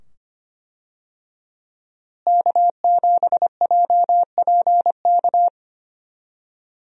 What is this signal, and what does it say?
Morse code, heard as a keyed tone, about 25 wpm, 700 Hz. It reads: K7JPK
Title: USA Call Signs - 5 Characters - Repeated 2x - 25wpm